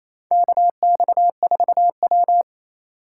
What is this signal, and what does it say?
Morse code, heard as a keyed tone, about 27 wpm, 700 Hz. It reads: KX4W
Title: USA Call Signs - 4 Characters - 4x Character Spacing - 28wpm